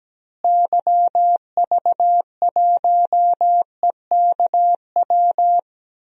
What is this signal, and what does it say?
Morse code, heard as a keyed tone, about 17 wpm, 700 Hz. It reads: YV1EKW
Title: DXCC-100 Call Signs - 6 Characters - Repeated 3x - 17wpm